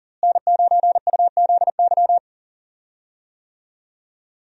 Morse code, heard as a keyed tone, about 40 wpm, 700 Hz. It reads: N9UZY
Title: USA Call Signs - 5 Characters - Repeated 3x - 40wpm